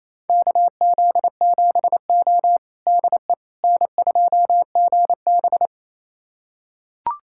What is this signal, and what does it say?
Morse code, heard as a keyed tone, about 27 wpm, 690 Hz. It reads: KZ7O DE N2GB
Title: USA Callsign DE Callsign - 4 Characters - 28wpm